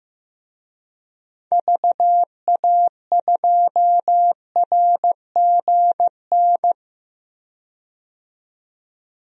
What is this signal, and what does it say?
Morse code, heard as a keyed tone, about 15 wpm, 700 Hz. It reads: VA2RGN
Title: DXCC-100 Call Signs - 6 Characters - 15wpm